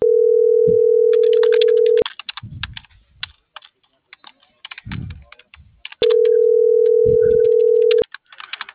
Ringback tone (North America), 440 + 480 Hz.